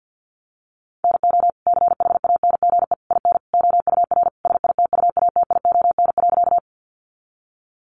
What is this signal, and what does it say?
Morse code, heard as a keyed tone, about 50 wpm, 700 Hz. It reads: NO CHANGE IN OUR SITUATION.
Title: Sentences from Top 1200 Words 50wpm